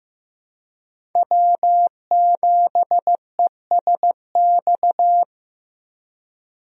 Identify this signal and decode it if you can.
Morse code, heard as a keyed tone, about 15 wpm, 700 Hz. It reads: W7ESX
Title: USA Call Signs - 5 Characters - 15wpm